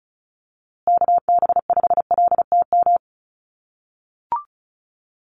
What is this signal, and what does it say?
Morse code, heard as a keyed tone, about 34 wpm, 700 Hz. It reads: KB5LTM
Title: USA Call Signs Repeated 2x 35wpm